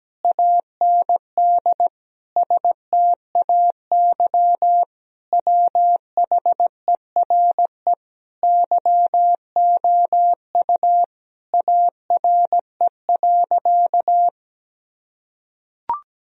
Morse code, heard as a keyed tone, about 17 wpm, 700 Hz. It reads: AND STAY WHERE YOU ARE.
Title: Sentences from Top 1000 Words - Review Part 1 - 17wpm